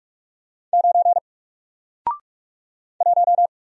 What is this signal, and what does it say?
Morse code, heard as a keyed tone, about 44 wpm, 700 Hz. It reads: 9 1
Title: Single Character (Letter:Number:Punctuation:Prosign) 45wpm